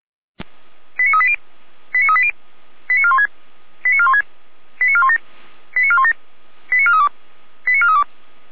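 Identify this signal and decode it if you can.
Five-tone selective call (ZVEI), heard as quick runs of five tones, tones of about 80 ms. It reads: 89380 89380 89416 89416 89417 89417 89432 89432
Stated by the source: ZVEI five-tone selective call 89356 received by SDR.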